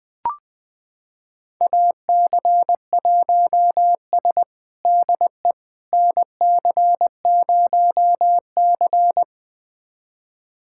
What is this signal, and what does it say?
Morse code, heard as a keyed tone, about 20 wpm, 700 Hz. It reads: AC1S DE NC0C
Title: USA Callsign DE Callsign - 4 Characters - 20wpm